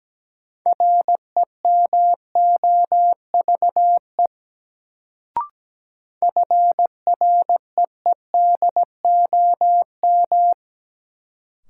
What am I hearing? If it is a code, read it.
Morse code, heard as a keyed tone, about 17 wpm, 700 Hz. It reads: REMOVE FREEDOM
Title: Sentences from Top 800 Words - New Word Drill - 17wpm